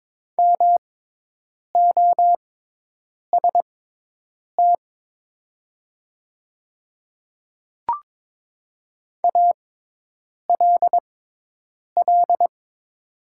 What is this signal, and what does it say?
Morse code, heard as a keyed tone, about 22 wpm, spaced as for 8 wpm, 700 Hz. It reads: MOST ALL
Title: 6x Character Spacing - Top 100 Words - 22wpm